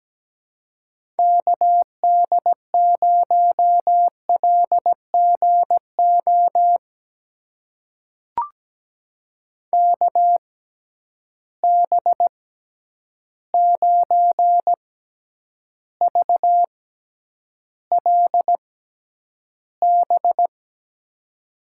Morse code, heard as a keyed tone, about 17 wpm, spaced as for 9 wpm, 700 Hz. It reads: KD0LGO KB9VLB
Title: USA Call Signs - 6 Characters - 6x Character Spacing - 17wpm